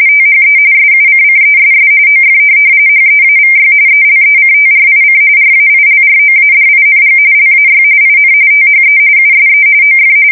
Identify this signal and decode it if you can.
RTTY, about 45.45 baud. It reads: THE SUCCESS OF HEAVY ELECTRIC RAILROADING LAY IN THE USE OF